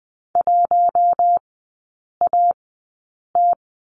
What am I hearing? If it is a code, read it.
Morse code, heard as a keyed tone, about 20 wpm, spaced as for 8 wpm, 700 Hz. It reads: 1AT
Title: 041 Lesson 8e - Number 1 - Three Characters 20wpm